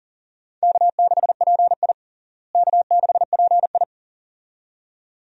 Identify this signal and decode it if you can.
Morse code, heard as a keyed tone, about 40 wpm, 700 Hz. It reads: K6PI K6PI
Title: USA Call Signs - 4 Characters - Repeated 2x - 40wpm